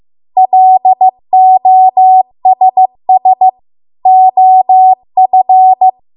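Morse code, heard as a keyed tone, about 15 wpm, 750 Hz. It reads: LOSS OF
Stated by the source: Morse code practice at 15 wpm from the national amateur radio association's code archive.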